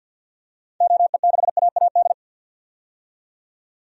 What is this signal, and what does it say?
Morse code, heard as a keyed tone, about 52 wpm, 700 Hz. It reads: OE6AAD
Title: DXCC-100 Call Signs - 6 Characters - 50wpm